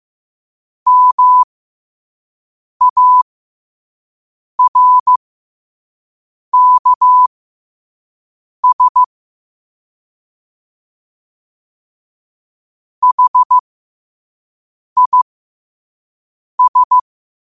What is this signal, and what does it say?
Morse code, heard as a keyed tone, about 15 wpm, spaced as for 5 wpm, 1000 Hz. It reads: MARKS HIS
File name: qotdfiles / 5wpm